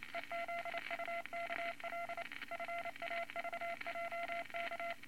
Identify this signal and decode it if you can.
Morse code, heard as a keyed tone, about 28 wpm, 680 Hz, with noise. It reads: EZUKL FAVJK
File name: Morse Code Records / 22 A LCM 172 Twenty Words Per Minute